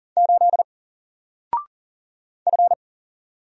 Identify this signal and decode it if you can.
Morse code, heard as a keyed tone, about 40 wpm, 700 Hz. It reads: 8 F
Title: Single Character (Letter:Number:Punctuation:Prosign) Alt 40wpm